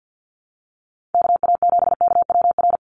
Morse code, heard as a keyed tone, about 50 wpm, 700 Hz. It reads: KA7KWR
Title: USA Call Signs 50wpm